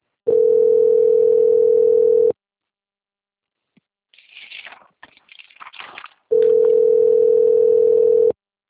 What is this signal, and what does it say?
Ringback tone (North America), 440 + 480 Hz.